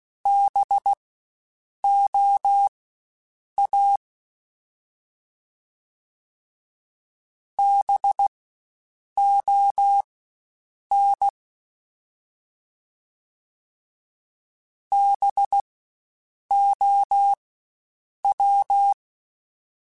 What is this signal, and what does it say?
Morse code, heard as a keyed tone, about 16 wpm, spaced as for 7 wpm, 800 Hz. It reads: BOA BON BOW
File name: Spaced Alphas / Spaced Alphas mp3s / Bs sp @15